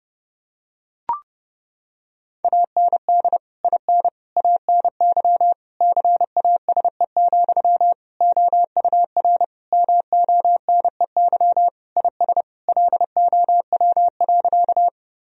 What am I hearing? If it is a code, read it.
Morse code, heard as a keyed tone, about 32 wpm, 700 Hz. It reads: AND IN ANY CASE, OUR MONEY IS LOW.
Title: Sentences from Top 1000 Words - Review Part 1 - 30wpm